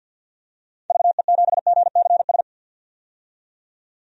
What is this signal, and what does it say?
Morse code, heard as a keyed tone, about 52 wpm, 700 Hz. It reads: VE7GCS
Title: DXCC-100 Call Signs - 6 Characters - 50wpm